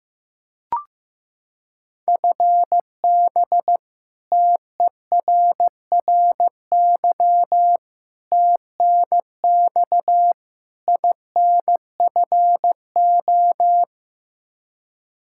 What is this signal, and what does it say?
Morse code, heard as a keyed tone, about 15 wpm, 700 Hz. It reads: FB TERRY TNX INFO
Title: Practice QSO - Exchange 3 - More Info - 15wpm